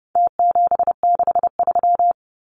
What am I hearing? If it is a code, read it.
Morse code, heard as a keyed tone, about 30 wpm, 700 Hz. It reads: T763
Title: Numbers - Contest Style - 4-Digit 30wpm